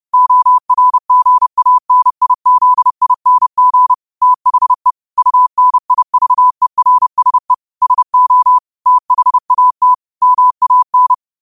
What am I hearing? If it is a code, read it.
Morse code, heard as a keyed tone, about 30 wpm, 1000 Hz. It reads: ORGANIZING THE UNIVERSE SO THAT MAN
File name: 30wpm